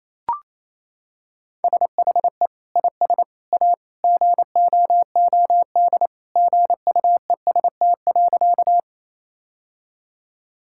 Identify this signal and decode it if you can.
Morse code, heard as a keyed tone, about 28 wpm, 700 Hz. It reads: SHE IS A GOOD GUEST.
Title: Sentences from Top 1600 Words 28wpm